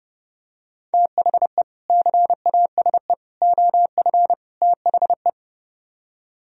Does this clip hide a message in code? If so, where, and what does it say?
Yes: Morse code from 0.9 to 5.3 s, heard as a keyed tone, about 30 wpm, 700 Hz. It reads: THE CASE OF THE